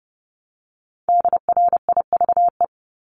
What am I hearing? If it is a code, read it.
Morse code, heard as a keyed tone, about 30 wpm, 700 Hz. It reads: DRIVE
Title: Top 500 Words Repeated 30wpm